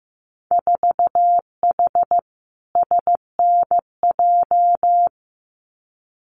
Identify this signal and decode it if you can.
Morse code, heard as a keyed tone, about 15 wpm, 700 Hz. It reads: 4H SNJ
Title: Winter Field Day Exchanges Encore 15wpm